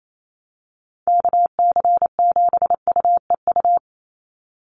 Morse code, heard as a keyed tone, about 28 wpm, 700 Hz. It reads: KC7UEU
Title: USA Call Signs 28wpm